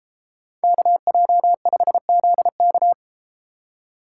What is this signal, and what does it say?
Morse code, heard as a keyed tone, about 33 wpm, 700 Hz. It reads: KJ5ZK